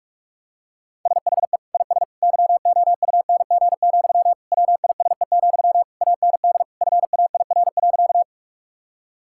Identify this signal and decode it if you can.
Morse code, heard as a keyed tone, about 46 wpm, 700 Hz. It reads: SHE IS YOUNG, WISE, AND FAIR.
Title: Sentences from Top 1600 Words 45wpm